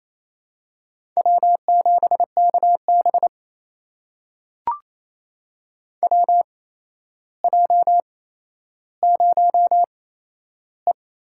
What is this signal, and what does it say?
Morse code, heard as a keyed tone, about 28 wpm, spaced as for 12 wpm, 700 Hz. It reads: W7KB WJ0E
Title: USA Call Signs - 4 Characters - 8x Character Spacing - 28wpm